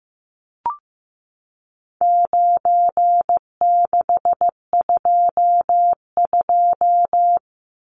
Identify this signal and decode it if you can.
Morse code, heard as a keyed tone, about 15 wpm, 700 Hz. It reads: 9622
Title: Numbers - 4-Digit 15wpm